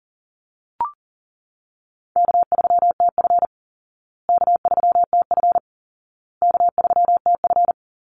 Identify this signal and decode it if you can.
Morse code, heard as a keyed tone, about 40 wpm, 700 Hz. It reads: K3TF K3TF K3TF
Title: Call Signs Repeated 2x - 40wpm